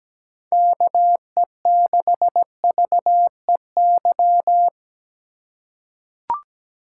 Morse code, heard as a keyed tone, about 17 wpm, 700 Hz. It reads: KE6VEY
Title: USA Call Signs - 2x Character Spacing - 17wpm